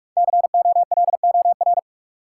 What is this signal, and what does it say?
Morse code, heard as a keyed tone, about 47 wpm, 700 Hz. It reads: COLOR
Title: Sentences from Top 1000 Words - Review Part 2 - New Word Drill - 45wpm